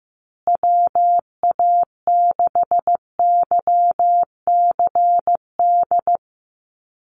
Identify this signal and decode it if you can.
Morse code, heard as a keyed tone, about 15 wpm, 700 Hz. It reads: WA6YCD
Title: USA Call Signs Repeated 2x 15wpm